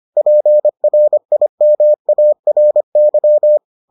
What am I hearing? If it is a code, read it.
Morse code, heard as a keyed tone, about 25 wpm, 600 Hz. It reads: PRIMARY